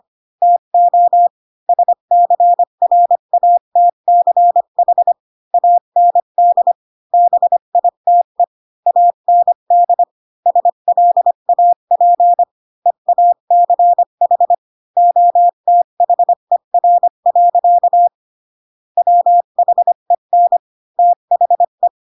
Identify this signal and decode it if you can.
Morse code, heard as a keyed tone, about 25 wpm, 700 Hz. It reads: TO SCRATCH AND BITE AND SLAP EACH OTHER. WHEN THE